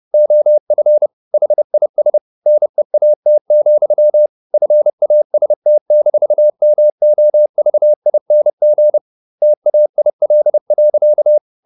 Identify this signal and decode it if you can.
Morse code, heard as a keyed tone, about 30 wpm, 600 Hz. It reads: OF HIS NEAT, FAST-MOVING TAIL.
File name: the old man and the sea / the old man and the sea 1 30wpm